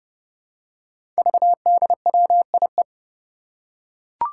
Morse code, heard as a keyed tone, about 30 wpm, 700 Hz. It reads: VDWIE
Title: Five Challenging Letter-Number Combinations 30wpm